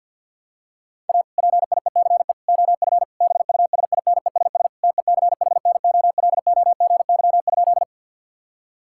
Morse code, heard as a keyed tone, about 52 wpm, 700 Hz. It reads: A PIECE OF BUSINESS TECHNOLOGY?
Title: Sentences from Top 1000 Words - Review Part 5 - 50wpm